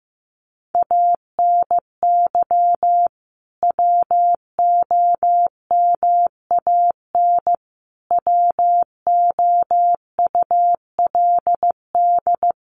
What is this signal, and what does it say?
Morse code, heard as a keyed tone, about 15 wpm, 700 Hz. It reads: ANY WOMAN WOULD
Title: Sentences from Top 1100 Words 15wpm